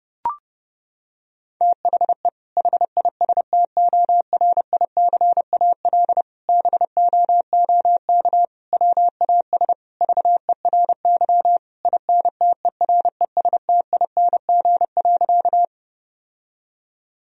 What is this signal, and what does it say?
Morse code, heard as a keyed tone, about 32 wpm, 700 Hz. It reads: THE HISTORICAL BOOK WAS VERY INTERESTING.
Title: Sentences from Top 1000 Words - Review Part 4 - 30wpm